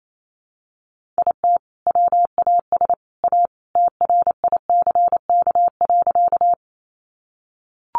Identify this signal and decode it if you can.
Morse code, heard as a keyed tone, about 28 wpm, 700 Hz. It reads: IT WAS A TRICK.
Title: Sentences from Top 1500 Words 28wpm